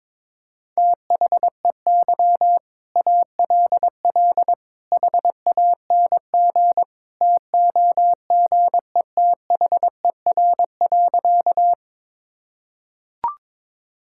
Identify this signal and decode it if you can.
Morse code, heard as a keyed tone, about 21 wpm, 700 Hz. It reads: THEY ALL HANG TOGETHER.
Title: Sentences from Top 1600 Words 22wpm